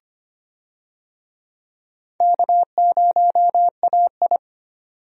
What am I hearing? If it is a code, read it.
Morse code, heard as a keyed tone, about 25 wpm, 700 Hz. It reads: K0AI
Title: USA Call Signs - 4 Characters - 25wpm